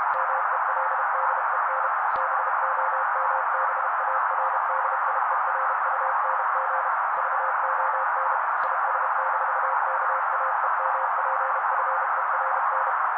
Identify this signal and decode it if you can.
Morse code, heard as a keyed tone, about 32 wpm, 530 Hz. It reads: MELDEN BOMBARDIERUNG VON IS-KAEMPFERN
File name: daily / 31wpm noise